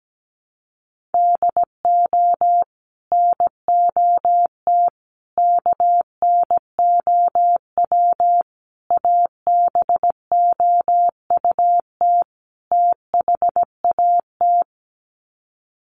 Morse code, heard as a keyed tone, about 17 wpm, 700 Hz. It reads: DO NOT KNOW ABOUT THAT
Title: Sets of 5 Words Encore2 17wpm